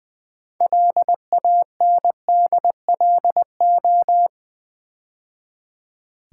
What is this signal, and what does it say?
Morse code, heard as a keyed tone, about 20 wpm, 700 Hz. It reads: LANDLO
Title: Build A Word - 2 Letters At A Time - 8-Letter Words - 20wpm